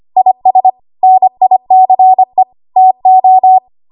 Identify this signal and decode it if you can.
Morse code, heard as a keyed tone, about 25 wpm, 750 Hz. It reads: IS NICE TO
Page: 25 WPM Code Archive